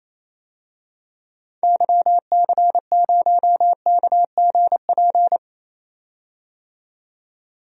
Morse code, heard as a keyed tone, about 28 wpm, 700 Hz. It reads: YC0KGP
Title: DXCC-100 Call Signs - 6 Characters - 28wpm